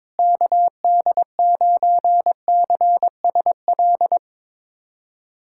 Morse code, heard as a keyed tone, about 22 wpm, 700 Hz. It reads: KD9CSL
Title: USA Callsign DE Callsign - 6 Characters - 22wpm